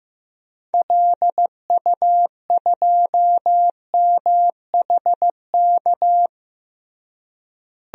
Morse code, heard as a keyed tone, about 16 wpm, 700 Hz. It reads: LU2MHK
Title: DXCC-100 Call Signs - 6 Characters - Repeated 3x - 15wpm